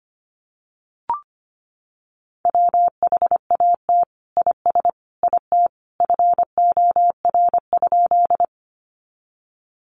Morse code, heard as a keyed tone, about 25 wpm, 700 Hz. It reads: WHAT IS IT FOR?
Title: Sentences from Top 100 Words 25wpm